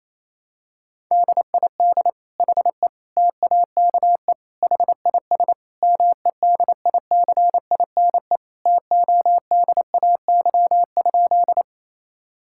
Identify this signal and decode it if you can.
Morse code, heard as a keyed tone, about 27 wpm, 700 Hz. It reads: DID HE TAKE HIS MEDICINE TODAY?
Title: Sentences from Top 1000 Words - Review Part 6 - 28wpm